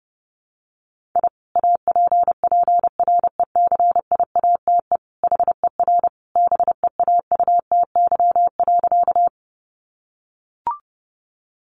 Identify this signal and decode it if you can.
Morse code, heard as a keyed tone, about 30 wpm, 700 Hz. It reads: I APPRECIATE HER BEAUTY.
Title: Sentences from Top 1300 Words 30wpm